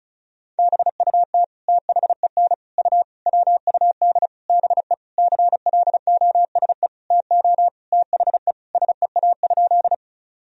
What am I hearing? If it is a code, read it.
Morse code, heard as a keyed tone, about 34 wpm, 700 Hz. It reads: BUT THEN U WUD BE CLOSE TO THE SEA?
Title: Sentences from Top 300 Words and Abbreviations 35wpm